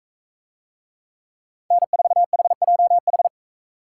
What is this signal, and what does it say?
Morse code, heard as a keyed tone, about 44 wpm, 700 Hz. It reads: N4HJH